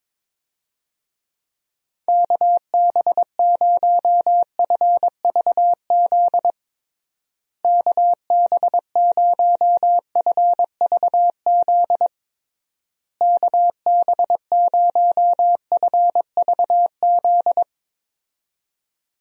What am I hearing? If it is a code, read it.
Morse code, heard as a keyed tone, about 22 wpm, 700 Hz. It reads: KB0FVZ KB0FVZ KB0FVZ
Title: USA Call Signs - 6 Characters - Repeated 3x - 22wpm